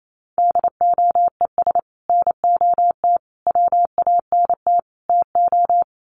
Morse code, heard as a keyed tone, about 28 wpm, 700 Hz. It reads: DOES NOT WANT TO
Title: Sets of 4 Words 28wpm